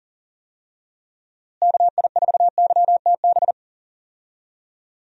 Morse code, heard as a keyed tone, about 40 wpm, 700 Hz. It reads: KI4YTB
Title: USA Call Signs - 6 Characters - Repeated 2x - 40wpm